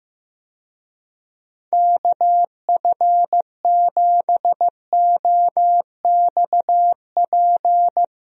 Morse code, heard as a keyed tone, about 16 wpm, 700 Hz. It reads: KF7OXP
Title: USA Call Signs - 6 Characters - 15wpm